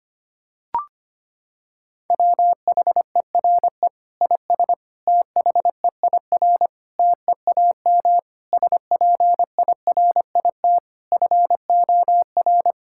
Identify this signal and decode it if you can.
Morse code, heard as a keyed tone, about 25 wpm, 700 Hz. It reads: WHERE IS THEIR TEAM SPIRIT FOR
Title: Sentences from Top 1600 Words 25wpm